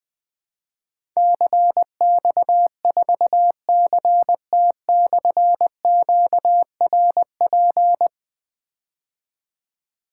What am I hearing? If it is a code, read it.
Morse code, heard as a keyed tone, about 20 wpm, 700 Hz. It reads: CX4CT/QRP
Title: Difficult Call Signs 20wpm